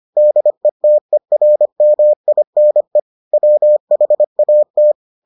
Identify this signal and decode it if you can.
Morse code, heard as a keyed tone, about 25 wpm, 600 Hz. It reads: DETERMINE WHAT